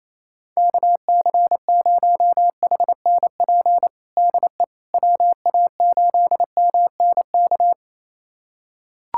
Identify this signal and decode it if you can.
Morse code, heard as a keyed tone, about 28 wpm, 700 Hz. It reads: KC0HNP DE WA8MNK
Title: USA Callsign DE Callsign 28wpm